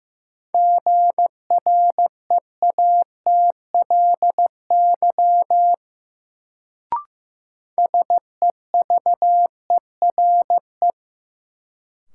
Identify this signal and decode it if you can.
Morse code, heard as a keyed tone, about 15 wpm, 700 Hz. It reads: GREATLY SEVERE
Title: Sentences from Top 1400 Words - New Word Drill - 15wpm